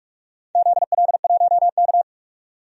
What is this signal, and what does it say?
Morse code, heard as a keyed tone, about 45 wpm, 700 Hz. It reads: ZL1K
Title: DXCC-100 Call Signs - 4 Characters - 45wpm